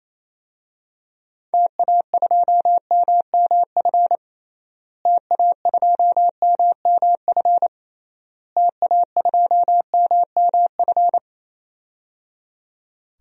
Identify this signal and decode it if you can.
Morse code, heard as a keyed tone, about 28 wpm, 700 Hz. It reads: TA2MMF TA2MMF TA2MMF
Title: DXCC-100 Call Signs - 6 Characters - Repeated 3x - 28wpm